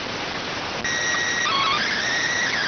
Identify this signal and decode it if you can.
SSTV (slow-scan television), PD 290.